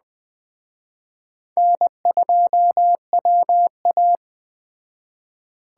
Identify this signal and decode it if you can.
Morse code, heard as a keyed tone, about 20 wpm, 700 Hz. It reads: N2WA
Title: Speed Racing - USA 1x2 Call Signs 17wpm